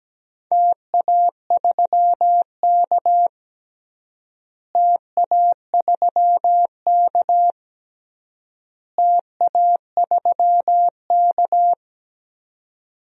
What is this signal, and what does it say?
Morse code, heard as a keyed tone, about 17 wpm, 700 Hz. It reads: TA3K TA3K TA3K
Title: DXCC-100 Call Signs - 4 Characters - Repeated 3x - 17wpm